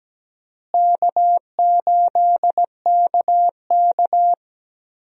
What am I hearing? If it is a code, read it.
Morse code, heard as a keyed tone, about 17 wpm, 700 Hz. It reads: K8KK
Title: USA Call Signs - 4 Characters - Repeated 2x - 17wpm